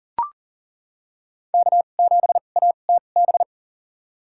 Morse code, heard as a keyed tone, about 40 wpm, spaced as for 29 wpm, 700 Hz. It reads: K7ATB
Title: USA Call Signs - 5 Characters - 2x Character Spacing - 40wpm